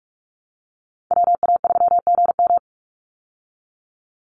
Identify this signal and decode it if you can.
Morse code, heard as a keyed tone, about 44 wpm, 700 Hz. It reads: WA3GM
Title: Call Signs Repeated 2x - 45wpm